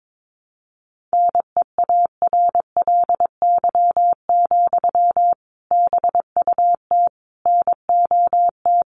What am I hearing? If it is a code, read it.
Morse code, heard as a keyed tone, about 22 wpm, 700 Hz. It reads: NEARLY, BUT NOT
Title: Sentences from Top 1300 Words 22wpm